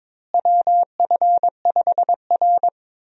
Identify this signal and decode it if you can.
Morse code, heard as a keyed tone, about 22 wpm, 700 Hz. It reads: WF5R
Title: USA Call Signs - 4 Characters - Repeated 3x - 22wpm